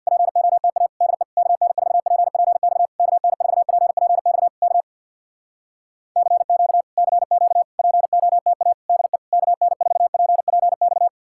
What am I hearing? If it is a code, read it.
Morse code, heard as a keyed tone, about 64 wpm, 700 Hz. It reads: POTA DE KN4PPX KN4PPX K CQ CQ POTA DE KN4PPX